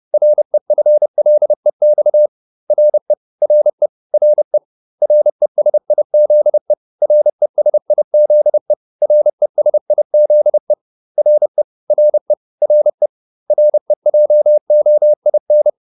Morse code, heard as a keyed tone, about 30 wpm, 600 Hz. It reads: REFLEX RE RE RE RESIZE RESIZE RESIZE RE RE RE REJOIN